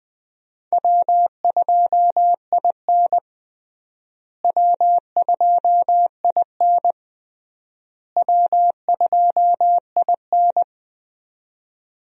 Morse code, heard as a keyed tone, about 20 wpm, 700 Hz. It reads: W2IN W2IN W2IN
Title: USA Call Signs - 4 Characters - Repeated 3x - 20wpm